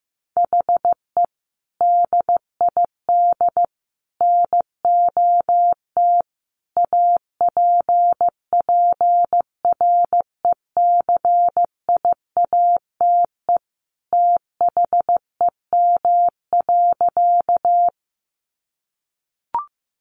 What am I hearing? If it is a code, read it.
Morse code, heard as a keyed tone, about 15 wpm, 700 Hz. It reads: HE DID NOT APPRECIATE THEM.
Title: Sentences from Top 1300 Words 15wpm